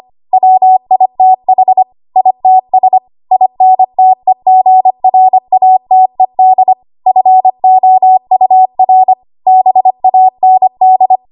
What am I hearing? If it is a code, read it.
Morse code, heard as a keyed tone, about 25 wpm, 750 Hz. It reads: WITH ITS INTEGRATED FOUR BAND
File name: Archive / 25 WPM / 25WPM